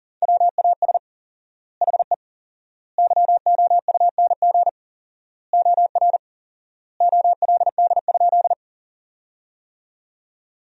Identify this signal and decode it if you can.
Morse code, heard as a keyed tone, about 40 wpm, 700 Hz. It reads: WAS HE YOUNG OR OLD?
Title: 4x Word-Spacing - Sentences from Top 400 Words - 40wpm